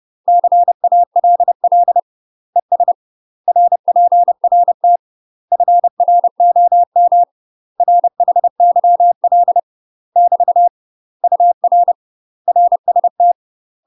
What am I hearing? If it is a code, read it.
Morse code, heard as a keyed tone, about 30 wpm, 700 Hz. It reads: CALL ES RPRT FROM RHYL <BT> UR RST
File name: UK MASTER QSOs TRAINING / 30WPM / UK MASTER QSOs 0005